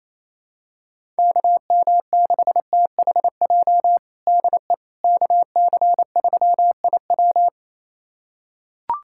Morse code, heard as a keyed tone, about 27 wpm, 700 Hz. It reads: KM6THJ DE KC3IW
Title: USA Callsign DE Callsign 28wpm